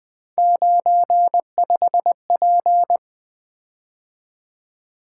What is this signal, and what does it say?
Morse code, heard as a keyed tone, about 20 wpm, 700 Hz. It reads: 95P
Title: Three Letter-Number Combinations - 20wpm